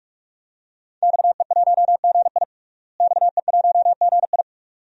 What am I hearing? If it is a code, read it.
Morse code, heard as a keyed tone, about 46 wpm, 700 Hz. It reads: XE1GI XE1GI
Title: DXCC-100 Call Signs - 5 Characters - Repeated 2x - 45wpm